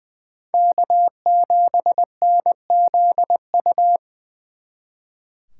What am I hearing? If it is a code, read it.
Morse code, heard as a keyed tone, about 25 wpm, 700 Hz. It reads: K7NZU
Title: Speed Racing - USA 1x3 Call Signs 20wpm